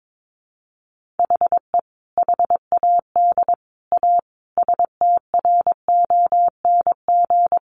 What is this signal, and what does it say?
Morse code, heard as a keyed tone, about 22 wpm, 700 Hz. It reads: HE HAD A STRONG
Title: Sentences from Top 1500 Words 22wpm